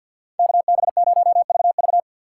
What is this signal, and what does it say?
Morse code, heard as a keyed tone, about 51 wpm, 700 Hz. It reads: KB0VV
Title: USA Call Signs - 5 Characters - 4x Character Spacing - 50wpm